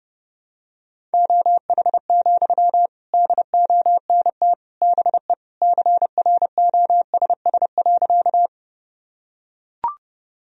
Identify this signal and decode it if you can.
Morse code, heard as a keyed tone, about 30 wpm, 700 Hz. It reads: OH, DONT BE CROSS.
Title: Sentences from Top 1000 Words - Review Part 1 - 30wpm